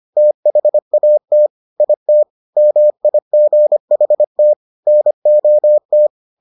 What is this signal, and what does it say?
Morse code, heard as a keyed tone, about 25 wpm, 600 Hz. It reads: THAT IT MIGHT NOT